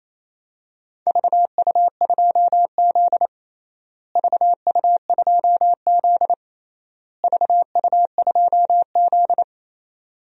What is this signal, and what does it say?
Morse code, heard as a keyed tone, about 28 wpm, 700 Hz. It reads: VU2Z VU2Z VU2Z
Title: DXCC-100 Call Signs - 4 Characters - Repeated 3x - 28wpm